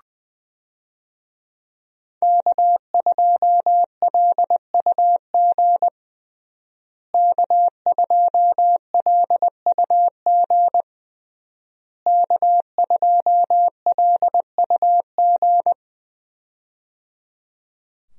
Morse code, heard as a keyed tone, about 20 wpm, 700 Hz. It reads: K2LUG K2LUG K2LUG
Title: USA Call Signs - 5 Characters - Repeated 3x - 20wpm